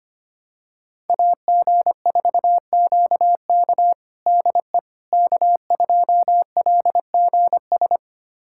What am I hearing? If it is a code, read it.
Morse code, heard as a keyed tone, about 25 wpm, 700 Hz. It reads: AG4QK DE K2LGS
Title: USA Callsign DE Callsign - 5 Characters - 25wpm